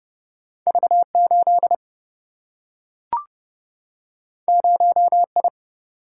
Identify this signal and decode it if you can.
Morse code, heard as a keyed tone, about 30 wpm, 700 Hz. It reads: V8 0I
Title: Single-Letter Single-Number 30wpm